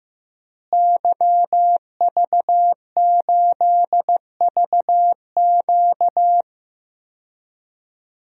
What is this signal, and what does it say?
Morse code, heard as a keyed tone, about 15 wpm, 700 Hz. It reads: YV8VQ
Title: DXCC-100 Call Signs - 5 Characters - Repeated 2x - 15wpm